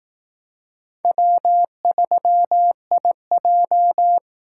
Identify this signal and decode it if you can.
Morse code, heard as a keyed tone, about 19 wpm, 700 Hz. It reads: W3IJ
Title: Speed Racing - USA Call Signs 15wpm